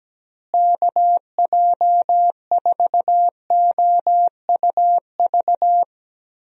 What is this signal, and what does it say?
Morse code, heard as a keyed tone, about 17 wpm, 700 Hz. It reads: KJ4OUV
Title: USA Call Signs - 6 Characters - Repeated 3x - 17wpm